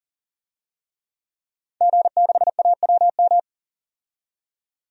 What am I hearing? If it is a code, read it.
Morse code, heard as a keyed tone, about 40 wpm, 700 Hz. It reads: G6AWM
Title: DXCC-100 Call Signs - 5 Characters - 40wpm